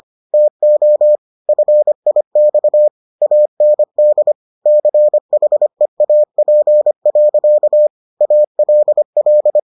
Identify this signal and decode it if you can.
Morse code, heard as a keyed tone, about 25 wpm, 600 Hz. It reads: TO FIX AND CHEAP. ALL